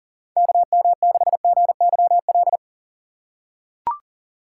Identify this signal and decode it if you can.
Morse code, heard as a keyed tone, about 40 wpm, 700 Hz. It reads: KM6GYL
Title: USA Call Signs - 6 Characters - 3x Character Spacing - 40wpm